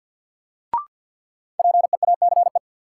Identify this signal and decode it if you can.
Morse code, heard as a keyed tone, about 52 wpm, 700 Hz. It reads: PEACE
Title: Sentences from Top 1500 Words - New Word Drill Alt - 50wpm